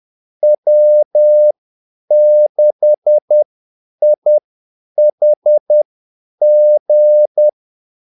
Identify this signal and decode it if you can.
Morse code, heard as a keyed tone, about 10 wpm, spaced as for 8 wpm, 600 Hz. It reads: W6IHG
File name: Callsigns-10-8wpm